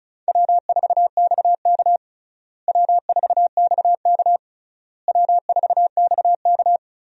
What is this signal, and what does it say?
Morse code, heard as a keyed tone, about 36 wpm, 700 Hz. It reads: W4XK W4XK W4XK
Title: USA Call Signs - 4 Characters - Repeated 3x - 35wpm